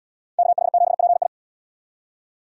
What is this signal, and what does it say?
Morse code, heard as a keyed tone, about 71 wpm, 700 Hz. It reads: KH7PI